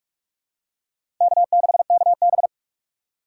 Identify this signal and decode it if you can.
Morse code, heard as a keyed tone, about 46 wpm, 700 Hz. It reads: K6KB
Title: USA Call Signs - 4 Characters - 6x Character Spacing - 45wpm